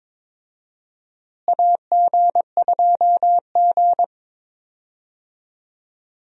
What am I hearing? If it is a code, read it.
Morse code, heard as a keyed tone, about 22 wpm, 700 Hz. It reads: AG2G
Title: USA Call Signs - 4 Characters - Repeated 3x - 22wpm